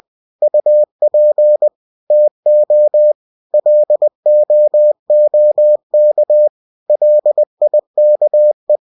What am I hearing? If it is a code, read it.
Morse code, heard as a keyed tone, about 20 wpm, 600 Hz. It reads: UP TO LOOK LIKE
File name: cw-books / winnie-the-pooh / winnie-20wpm-page-0072